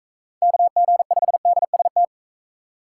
Morse code, heard as a keyed tone, about 38 wpm, 700 Hz. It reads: KG5DST